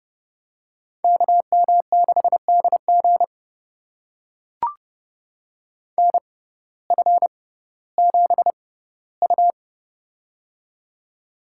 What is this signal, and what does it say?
Morse code, heard as a keyed tone, about 30 wpm, spaced as for 17 wpm, 700 Hz. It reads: KM6DG NF7U
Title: USA Call Signs - 6x Character Spacing - 30wpm